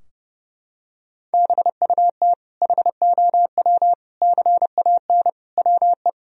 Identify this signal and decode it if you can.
Morse code, heard as a keyed tone, about 30 wpm, 700 Hz. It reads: BUT HOW CAN WE
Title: Sentences from Top 1000 Words - Review Part 4 - 30wpm